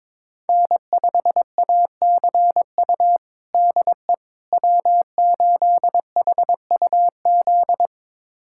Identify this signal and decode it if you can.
Morse code, heard as a keyed tone, about 22 wpm, 700 Hz. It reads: N5ACU DE W8HUZ
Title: USA Callsign DE Callsign - 5 Characters - 22wpm